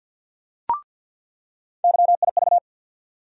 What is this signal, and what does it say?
Morse code, heard as a keyed tone, about 48 wpm, 700 Hz. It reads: YIV
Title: USA Call Sign Postfixes - 3 Letters 50wpm